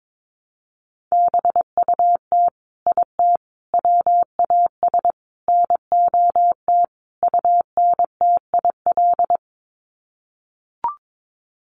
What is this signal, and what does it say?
Morse code, heard as a keyed tone, about 22 wpm, 700 Hz. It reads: BUT IT WAS NOT UNTIL
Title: Sets of 5 Words Encore2 22wpm